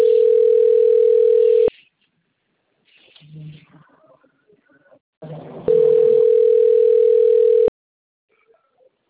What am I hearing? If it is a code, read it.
Ringback tone (North America), 440 + 480 Hz.